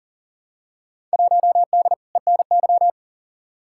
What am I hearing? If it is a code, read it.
Morse code, heard as a keyed tone, about 40 wpm, 700 Hz. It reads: 1D ENY
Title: Field Day Exchanges Encore - 40wpm